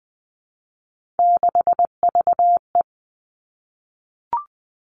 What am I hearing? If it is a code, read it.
Morse code, heard as a keyed tone, about 20 wpm, 700 Hz. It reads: 6VE
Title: Three Challenging Letter-Number Combinations 20wpm